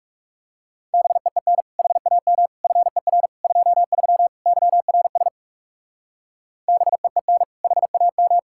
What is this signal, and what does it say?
Morse code, heard as a keyed tone, about 39 wpm, 700 Hz. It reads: BEEN HAM FER 23 YRS BEEN HAM